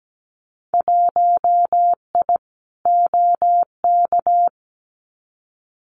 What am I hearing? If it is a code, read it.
Morse code, heard as a keyed tone, about 17 wpm, 700 Hz. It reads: 1I OK
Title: Winter Field Day Exchanges Encore 17wpm